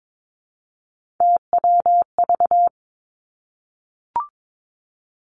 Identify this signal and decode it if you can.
Morse code, heard as a keyed tone, about 22 wpm, 700 Hz. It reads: TWV
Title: Three Challenging Letter-Number Combinations 22wpm